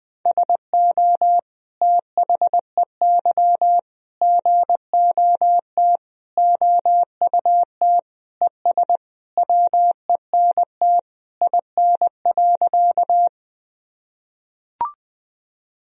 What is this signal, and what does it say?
Morse code, heard as a keyed tone, about 20 wpm, 700 Hz. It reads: SO THEY GOT OUT ES WENT IN.
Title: Sentences from Top 300 Words and Abbreviations 20wpm